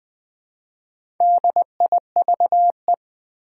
Morse code, heard as a keyed tone, about 20 wpm, 700 Hz. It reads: DIVE
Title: Build A Word - 1 Letter At A Time - 3 to 5-Letter Words - 20wpm